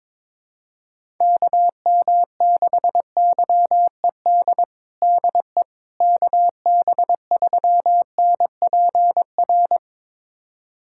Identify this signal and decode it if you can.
Morse code, heard as a keyed tone, about 22 wpm, 700 Hz. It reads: KM6YED DE KB3NPR
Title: USA Callsign DE Callsign - 6 Characters - 22wpm